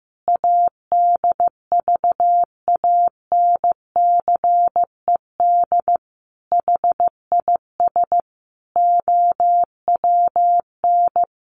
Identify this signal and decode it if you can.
Morse code, heard as a keyed tone, about 15 wpm, 700 Hz. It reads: ADVANCED HIS OWN
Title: Sentences from Top 1200 Words 15wpm